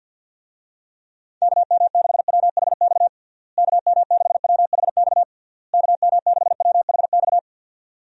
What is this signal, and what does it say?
Morse code, heard as a keyed tone, about 50 wpm, 700 Hz. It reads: KM6WHX KM6WHX KM6WHX
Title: USA Call Signs - 6 Characters - Repeated 3x - 50wpm